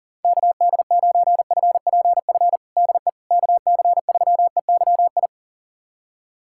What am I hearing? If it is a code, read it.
Morse code, heard as a keyed tone, about 37 wpm, 700 Hz. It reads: KD9FPF DE KC3EYI
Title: USA Callsign DE Callsign - 6 Characters - 40wpm